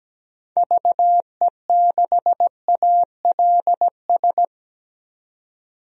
Morse code, heard as a keyed tone, about 17 wpm, 700 Hz. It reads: VE6ALS
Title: DXCC-100 Call Signs - 6 Characters - 17wpm